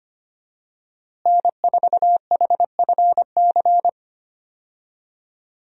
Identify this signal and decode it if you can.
Morse code, heard as a keyed tone, about 25 wpm, 700 Hz. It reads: N4HFC
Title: USA Call Signs - 5 Characters - Repeated 3x - 25wpm